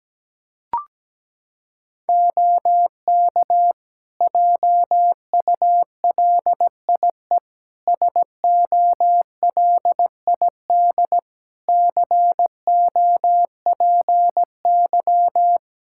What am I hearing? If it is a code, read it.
Morse code, heard as a keyed tone, about 17 wpm, 700 Hz. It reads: OK JULIE SOLID COPY
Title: Practice QSO - Exchange 3 - More Info - 17wpm